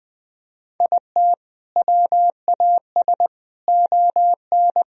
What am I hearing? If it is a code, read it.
Morse code, heard as a keyed tone, about 22 wpm, 700 Hz. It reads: IT WAS ON